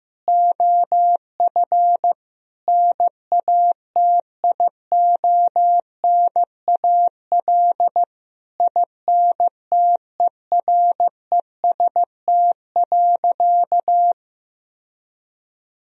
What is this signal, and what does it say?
Morse code, heard as a keyed tone, about 16 wpm, 700 Hz. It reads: OF NATIONAL INTEREST.
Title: Sentences from Top 1000 Words - Review Part 4 - 15wpm